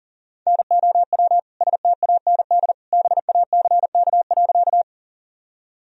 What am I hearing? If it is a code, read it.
Morse code, heard as a keyed tone, about 40 wpm, 700 Hz. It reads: NOW STAND BACK.
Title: Sentences from Top 1000 Words - Review Part 1 - 40wpm